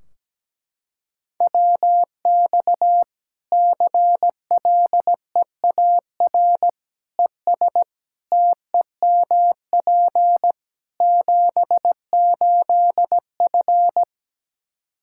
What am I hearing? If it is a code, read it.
Morse code, heard as a keyed tone, about 17 wpm, 700 Hz. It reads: WX CLEAR ES TEMP 78F
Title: K1USN SST QSO Practice 17wpm